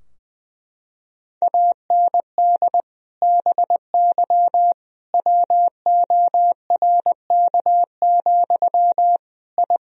Morse code, heard as a keyed tone, about 20 wpm, 700 Hz. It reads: AND BY WORK, I
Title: Sentences from Top 1000 Words - Review Part 4 - 20wpm